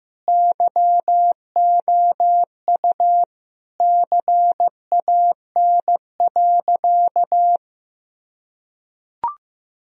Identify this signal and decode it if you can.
Morse code, heard as a keyed tone, about 15 wpm, 700 Hz. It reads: YOU CAN.
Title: Sentences from Top 1000 Words - Review Part 1 - 15wpm